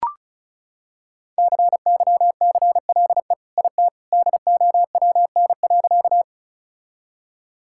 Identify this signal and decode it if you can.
Morse code, heard as a keyed tone, about 34 wpm, 700 Hz. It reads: CYCLE IT DOWN.
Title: Sentences from Top 1000 Words - Review Part 6 - 35wpm